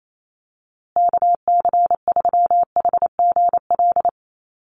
Morse code, heard as a keyed tone, about 28 wpm, 700 Hz. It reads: KC3HGL
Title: USA Call Signs Repeated 2x 28wpm